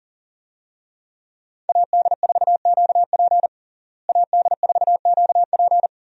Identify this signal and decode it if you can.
Morse code, heard as a keyed tone, about 38 wpm, 700 Hz. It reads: AD4QP AD4QP
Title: USA Call Signs - 5 Characters - Repeated 3x - 40wpm